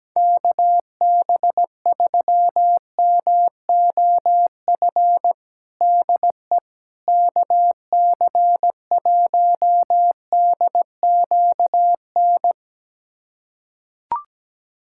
Morse code, heard as a keyed tone, about 17 wpm, 700 Hz. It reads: KB3MOF DE KC1DQN
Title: USA Callsign DE Callsign 17wpm